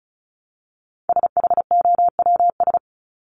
Morse code, heard as a keyed tone, about 34 wpm, 700 Hz. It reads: SHOWS
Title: ES:S Suffix - Top 100 Words with Base Words 35wpm